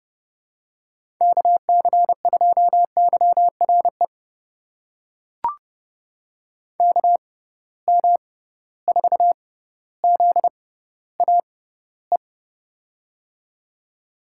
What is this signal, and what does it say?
Morse code, heard as a keyed tone, about 30 wpm, spaced as for 15 wpm, 700 Hz. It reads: KC2YRE KM4ZAE
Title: USA Call Signs - 6 Characters - 6x Character Spacing - 30wpm